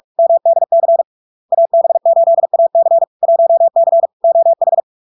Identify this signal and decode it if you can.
Morse code, heard as a keyed tone, about 46 wpm, 670 Hz. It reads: MDC AB8AC 1C OH
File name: LARK-FD-2022-Log-in-CW-45wpm-670-Hz